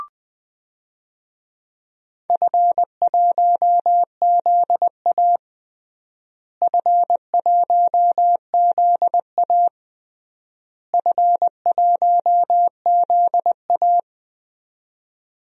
Morse code, heard as a keyed tone, about 20 wpm, 690 Hz. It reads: F1ZA F1ZA F1ZA
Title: DXCC-100 Call Signs - 4 Characters - Repeated 3x - 20wpm